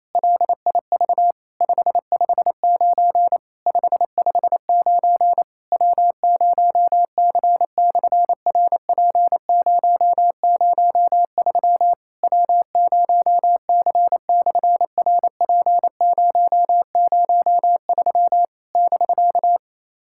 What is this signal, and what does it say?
Morse code, heard as a keyed tone, about 27 wpm, 700 Hz. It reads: LIV 559 559 W0C/RP003 W0C/RP003 <BK>
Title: SOTA QSO Practice 28wpm